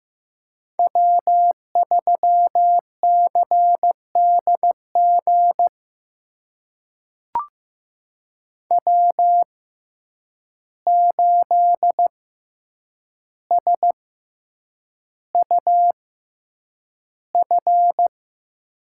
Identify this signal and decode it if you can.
Morse code, heard as a keyed tone, about 15 wpm, spaced as for 8 wpm, 700 Hz. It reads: W3CDG W8SUF
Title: USA Call Signs - 5 Characters - 6x Character Spacing - 15wpm